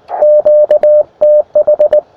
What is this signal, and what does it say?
Morse code, heard as a keyed tone, about 19 wpm, 600 Hz. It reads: QTH
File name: attachments / us7ign a 20m